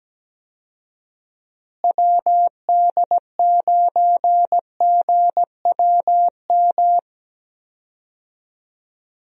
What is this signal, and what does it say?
Morse code, heard as a keyed tone, about 17 wpm, 700 Hz. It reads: WD9GWM
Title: USA Call Signs - 6 Characters - Repeated 2x - 17wpm